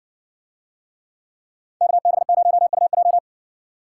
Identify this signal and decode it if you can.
Morse code, heard as a keyed tone, about 60 wpm, 700 Hz. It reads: KB0UJ